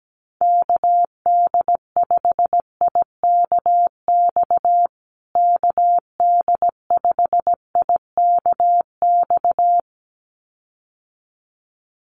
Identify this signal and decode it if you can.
Morse code, heard as a keyed tone, about 17 wpm, 700 Hz. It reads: KD5IKX KD5IKX
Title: USA Call Signs Repeated 2x 17wpm